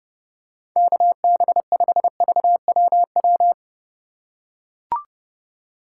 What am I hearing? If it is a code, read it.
Morse code, heard as a keyed tone, about 30 wpm, 700 Hz. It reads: KB5VWW
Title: USA Call Signs - 6 Characters - 4x Character Spacing - 30wpm